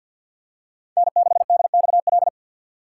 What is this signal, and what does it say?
Morse code, heard as a keyed tone, about 47 wpm, 700 Hz. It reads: N6DXL
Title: USA Call Signs - 5 Characters - 3x Character Spacing - 50wpm